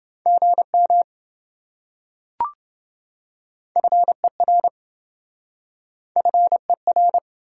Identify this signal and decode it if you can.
Morse code, heard as a keyed tone, about 27 wpm, 700 Hz. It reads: GM FER FER